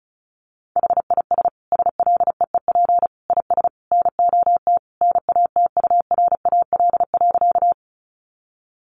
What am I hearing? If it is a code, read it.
Morse code, heard as a keyed tone, about 34 wpm, 700 Hz. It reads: HIS SLEEP IS NOT NATURAL.
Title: Sentences from Top 700 Words 35wpm